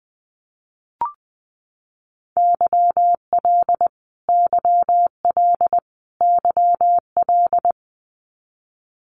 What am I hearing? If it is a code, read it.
Morse code, heard as a keyed tone, about 20 wpm, 700 Hz. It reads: YL YL YL
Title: Abbreviations Repeated 3x - 20wpm